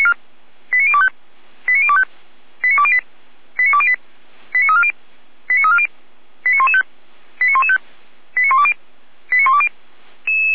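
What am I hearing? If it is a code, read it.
Five-tone selective call (ZVEI), heard as quick runs of five tones, tones of about 70 ms. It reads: 89025 89025 89208 89208 89340 89340 89115 89115 89120 89120
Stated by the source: ZVEI five-tone selective call received on an SDR.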